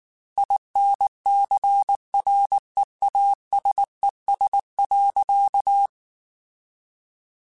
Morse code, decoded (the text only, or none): INCREASES.